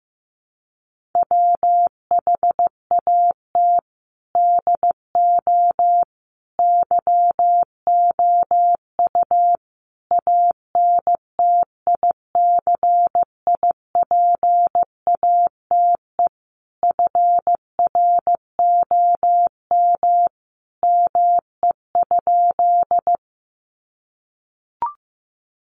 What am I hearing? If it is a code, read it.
Morse code, heard as a keyed tone, about 15 wpm, 700 Hz. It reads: WHAT DO YOU ANTICIPATE FROM ME?
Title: Sentences from Top 1500 Words 15wpm